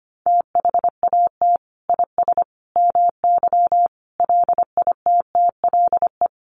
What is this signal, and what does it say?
Morse code, heard as a keyed tone, about 25 wpm, 700 Hz. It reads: THAT IS MY LITTLE
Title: Sentences from Top 1500 Words 25wpm